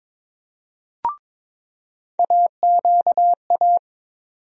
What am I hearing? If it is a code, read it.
Morse code, heard as a keyed tone, about 22 wpm, 700 Hz. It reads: AQA
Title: USA Call Sign Postfixes - 3 Letters 22wpm